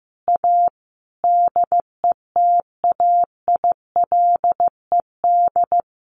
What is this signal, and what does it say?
Morse code, heard as a keyed tone, about 15 wpm, 700 Hz. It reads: A DETAILED
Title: Sentences from Top 1500 Words 15wpm